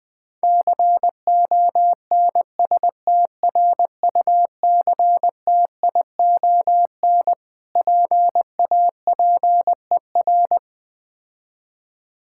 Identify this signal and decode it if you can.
Morse code, heard as a keyed tone, about 20 wpm, 690 Hz. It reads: CONSTRUCTION PAPER
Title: Difficult Words - Speed-Racing with Hints - 20wpm